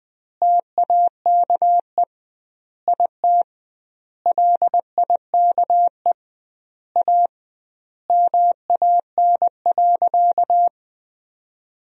Morse code, heard as a keyed tone, about 20 wpm, 700 Hz. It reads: TAKE IT LIKE A MAN.
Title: 2x Word-Spacing - Sentences from Top 300 Words - 20wpm